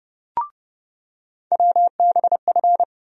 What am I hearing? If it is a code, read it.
Morse code, heard as a keyed tone, about 32 wpm, 700 Hz. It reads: WBF
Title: USA Call Sign Postfixes - 3 Letters 30wpm